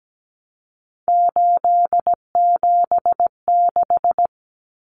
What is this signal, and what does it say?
Morse code, heard as a keyed tone, about 17 wpm, 700 Hz. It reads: 876
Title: Numbers - 3-Digit 17wpm